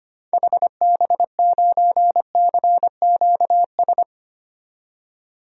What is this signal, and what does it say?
Morse code, heard as a keyed tone, about 25 wpm, 700 Hz. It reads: HB9CQS
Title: DXCC-100 Call Signs - 6 Characters - 25wpm